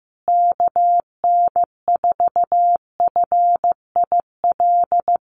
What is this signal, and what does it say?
Morse code, heard as a keyed tone, about 15 wpm, 700 Hz. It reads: KN4FIL
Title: USA Call Signs Repeated 3x 15wpm